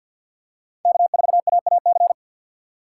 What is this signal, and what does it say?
Morse code, heard as a keyed tone, about 50 wpm, 700 Hz. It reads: K4AAC
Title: K1USN SST QSO Practice - 50wpm